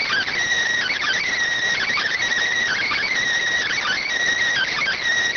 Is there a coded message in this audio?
Yes: Emergency Alert System (SAME) tones from 0.9 to 5.0 s, heard as harsh buzzing bursts.